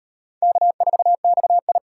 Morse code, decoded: K4XI

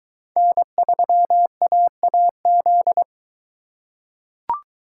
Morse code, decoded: N3AAZ

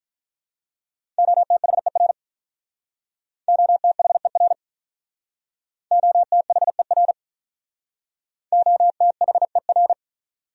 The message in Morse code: OTHER OTHER OTHER OTHER

